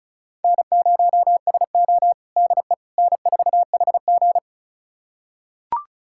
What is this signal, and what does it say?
Morse code, heard as a keyed tone, about 36 wpm, 700 Hz. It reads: N0SO DE N4HG
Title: USA Callsign DE Callsign - 4 Characters - 35wpm